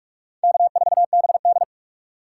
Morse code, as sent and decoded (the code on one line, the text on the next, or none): -.- ....- -... -..
K4BD